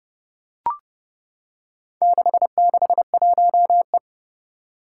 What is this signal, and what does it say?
Morse code, heard as a keyed tone, about 30 wpm, 700 Hz. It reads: 661E
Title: Four Challenging Letter-Number Combinations 30wpm